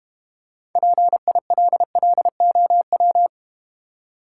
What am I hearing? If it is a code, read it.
Morse code, heard as a keyed tone, about 32 wpm, 700 Hz. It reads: PILLOW